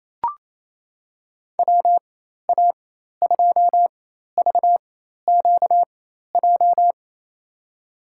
Morse code, heard as a keyed tone, about 28 wpm, spaced as for 13 wpm, 700 Hz. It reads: WA2VQJ
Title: USA Call Signs - 4x Character Spacing - 28wpm